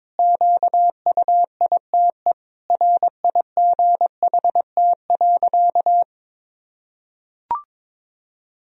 Morse code, heard as a keyed tone, about 22 wpm, 700 Hz. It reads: QUITE RIGHT.
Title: Sentences from Top 1000 Words - Review Part 2 - 22wpm